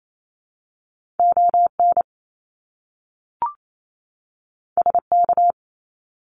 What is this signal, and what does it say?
Morse code, heard as a keyed tone, about 28 wpm, 700 Hz. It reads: ON SK
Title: Canadian Provinces and Territories 28wpm